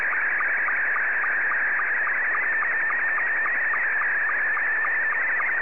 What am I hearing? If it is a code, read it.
SSTV (slow-scan television), Scottie 2.